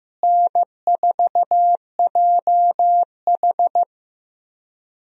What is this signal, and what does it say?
Morse code, heard as a keyed tone, about 15 wpm, 700 Hz. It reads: N4JH
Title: USA Call Signs - 4 Characters - Repeated 3x - 15wpm